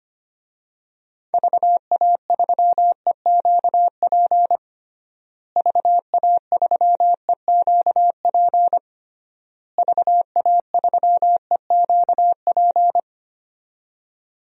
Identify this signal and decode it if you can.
Morse code, heard as a keyed tone, about 25 wpm, 700 Hz. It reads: VA3EQP VA3EQP VA3EQP
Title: DXCC-100 Call Signs - 6 Characters - Repeated 3x - 25wpm